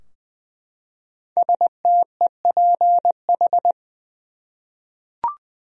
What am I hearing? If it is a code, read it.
Morse code, heard as a keyed tone, about 20 wpm, 700 Hz. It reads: STEPH
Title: USA Common Names - 20wpm